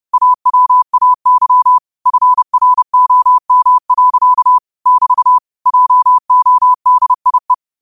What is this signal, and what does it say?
Morse code, heard as a keyed tone, about 30 wpm, 1000 Hz. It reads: AWAY FROM. <BT> JODIE
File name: qotdfiles / 30wpm